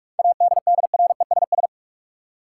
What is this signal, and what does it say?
Morse code, heard as a keyed tone, about 46 wpm, 700 Hz. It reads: ADDRESS
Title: Seven Letter Words - 45wpm